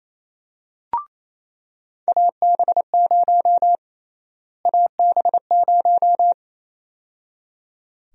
Morse code, heard as a keyed tone, about 28 wpm, 700 Hz. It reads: AB0 AB0
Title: USA Call Sign Prefixes - Repeated 2x - 28wpm